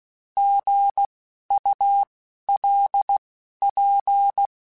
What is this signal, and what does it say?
Morse code, heard as a keyed tone, about 16 wpm, spaced as for 11 wpm, 800 Hz. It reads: GULP
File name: Alphabetical / Alphabetical mp3s / Gs @10